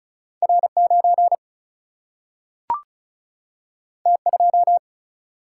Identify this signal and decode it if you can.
Morse code, heard as a keyed tone, about 36 wpm, 700 Hz. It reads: R 9 T 2
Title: Single-Letter Single-Number 35wpm